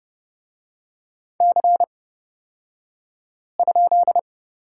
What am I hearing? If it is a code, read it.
Morse code, heard as a keyed tone, about 30 wpm, spaced as for 5 wpm, 700 Hz. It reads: C?
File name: Lesson 37d - Letter x - Two Characters 30wpm